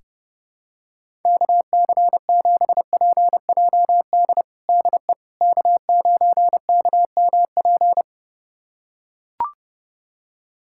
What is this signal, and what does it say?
Morse code, heard as a keyed tone, about 32 wpm, 700 Hz. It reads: KC7PJD DE K9KMP
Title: USA Callsign DE Callsign 30wpm